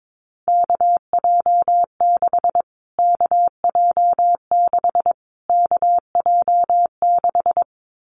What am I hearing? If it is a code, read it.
Morse code, heard as a keyed tone, about 22 wpm, 700 Hz. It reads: KJ6 KJ6 KJ6
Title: Callsign Prefixes for Top 25 DXCC - Repeated 3x - 22wpm